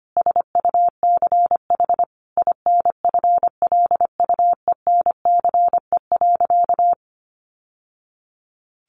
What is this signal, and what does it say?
Morse code, heard as a keyed tone, about 25 wpm, 700 Hz. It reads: SUCH INFLUENCE.
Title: Sentences from Top 1100 Words 25wpm